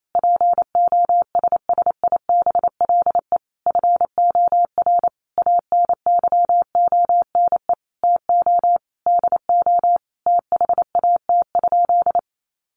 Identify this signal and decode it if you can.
Morse code, heard as a keyed tone, about 28 wpm, 700 Hz. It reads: POSSIBLE FOR ANYONE TO DO THAT?
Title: Sentences from Top 900 Words 28wpm